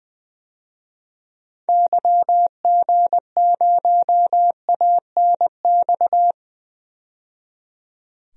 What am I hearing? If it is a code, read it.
Morse code, heard as a keyed tone, about 20 wpm, 700 Hz. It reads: YG0ANX
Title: DXCC-100 Call Signs - 6 Characters - 20wpm